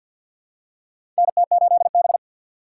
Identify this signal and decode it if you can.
Morse code, heard as a keyed tone, about 53 wpm, 700 Hz. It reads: NT8B
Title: USA Call Signs - 4x Character Spacing - 50wpm